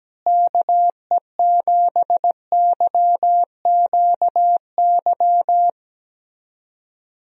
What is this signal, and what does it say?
Morse code, heard as a keyed tone, about 17 wpm, 700 Hz. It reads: KE7YQY
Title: USA Call Signs - 6 Characters - Repeated 3x - 17wpm